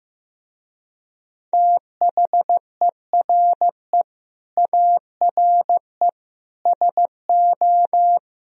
Morse code, heard as a keyed tone, about 16 wpm, 700 Hz. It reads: THERE ARE SO
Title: Speed-Racing - Sets of 4 Words 15wpm